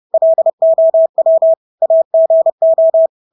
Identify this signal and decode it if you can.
Morse code, heard as a keyed tone, about 30 wpm, 650 Hz. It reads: LOW AGO
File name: Engelske ord 30-30 wpm